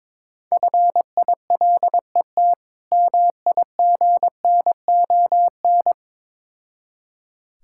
Morse code, heard as a keyed tone, about 22 wpm, 690 Hz. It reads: FILET MIGNON
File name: Difficult Words with Hints - 22wpm